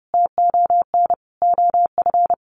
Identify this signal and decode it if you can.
Morse code, heard as a keyed tone, about 30 wpm, 700 Hz. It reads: TON OF
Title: A Sound Of Thunder 30wpm